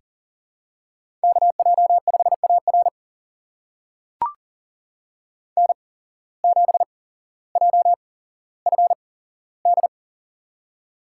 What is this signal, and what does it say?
Morse code, heard as a keyed tone, about 42 wpm, spaced as for 17 wpm, 700 Hz. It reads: KJ5AR N7JFD
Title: USA Call Signs - 5 Characters - 8x Character Spacing - 40wpm